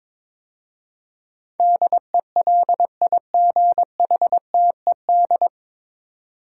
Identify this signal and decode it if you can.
Morse code, heard as a keyed tone, about 22 wpm, 700 Hz. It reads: DELIGHTED
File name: 9 Letter Words - First 3 Letters Given - 22wpm